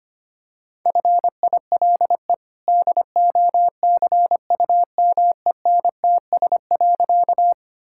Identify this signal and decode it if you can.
Morse code, heard as a keyed tone, about 25 wpm, 690 Hz. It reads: FILE DOCUMENTS.
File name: Easy Sentences with Hints - 25wpm